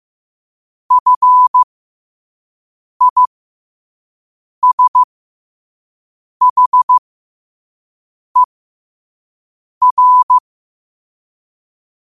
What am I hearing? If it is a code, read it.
Morse code, heard as a keyed tone, about 15 wpm, spaced as for 5 wpm, 1000 Hz. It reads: FISHER
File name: qotdfiles / 5wpm